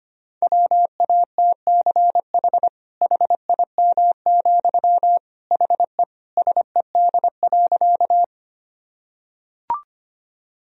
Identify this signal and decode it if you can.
Morse code, heard as a keyed tone, about 25 wpm, 700 Hz. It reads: WATCH HIM, HE SED.
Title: Sentences from Top 300 Words and Abbreviations 25wpm